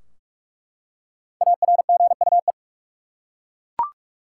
Morse code, heard as a keyed tone, about 46 wpm, 700 Hz. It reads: ARGUE
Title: Five Letter Words - 45wpm